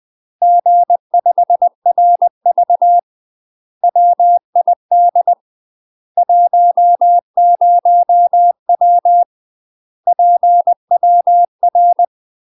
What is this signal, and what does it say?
Morse code, heard as a keyed tone, about 20 wpm, 700 Hz. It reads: G5RV WID 10W PWR
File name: UK MASTER QSOs TRAINING / 20WPM / UK MASTER QSOs 0048